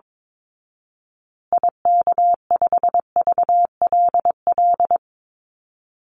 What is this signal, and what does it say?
Morse code, heard as a keyed tone, about 22 wpm, 690 Hz. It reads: IK5VLL
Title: Call Signs Repeated 2x - 22wpm